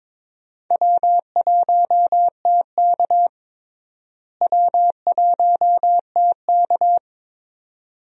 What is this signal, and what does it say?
Morse code, heard as a keyed tone, about 22 wpm, 700 Hz. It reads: W1TK W1TK
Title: USA Call Signs - 4 Characters - Repeated 2x - 22wpm